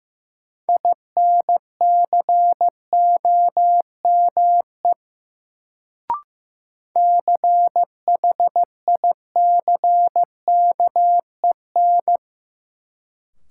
Morse code, heard as a keyed tone, about 15 wpm, 700 Hz. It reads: INCOME CHICKEN
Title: Sentences from Top 1000 Words - Review Part 5 - New Word Drill - 15wpm